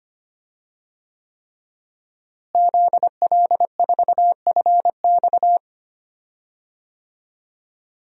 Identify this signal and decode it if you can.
Morse code, heard as a keyed tone, about 25 wpm, 700 Hz. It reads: ZL4FX
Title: DXCC-100 Call Signs - 5 Characters - Repeated 3x - 25wpm